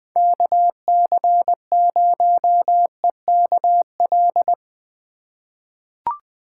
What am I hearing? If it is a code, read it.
Morse code, heard as a keyed tone, about 20 wpm, 700 Hz. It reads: KC0EKL
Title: USA Call Signs - 6 Characters - 3x Character Spacing - 20wpm